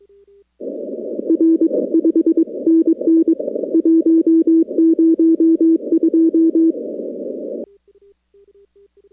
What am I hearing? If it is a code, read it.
Morse code, heard as a keyed tone, about 22 wpm, 330 Hz. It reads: R 5NN 102
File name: contests / 2019-cw-wpx-kt5j-k5pi- -2330 0-41